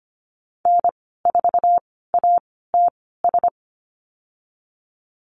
Morse code, heard as a keyed tone, about 25 wpm, spaced as for 16 wpm, 700 Hz. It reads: N4ATS
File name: Lesson 15g - Number 5 - Call Signs 25wpm